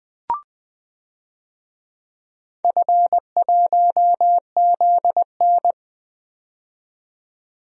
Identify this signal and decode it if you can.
Morse code, heard as a keyed tone, about 20 wpm, 700 Hz. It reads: F1ZN
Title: DXCC-100 Call Signs - 4 Characters - 20wpm